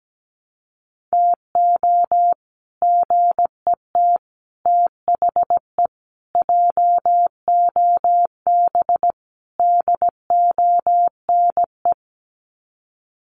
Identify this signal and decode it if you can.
Morse code, heard as a keyed tone, about 17 wpm, 700 Hz. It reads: TO GET THE JOB DONE
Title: Sets of 5 Words Encore2 17wpm